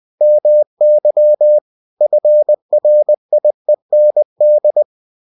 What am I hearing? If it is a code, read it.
Morse code, heard as a keyed tone, about 20 wpm, 600 Hz. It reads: MY FRIEND